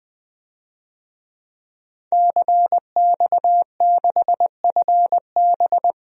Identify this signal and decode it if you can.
Morse code, heard as a keyed tone, about 20 wpm, 700 Hz. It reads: CX6FB
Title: DXCC-100 Call Signs - 5 Characters - 20wpm